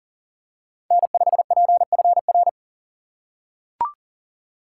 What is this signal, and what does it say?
Morse code, heard as a keyed tone, about 40 wpm, 700 Hz. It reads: N5PFR
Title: USA Call Signs - 5 Characters - 3x Character Spacing - 40wpm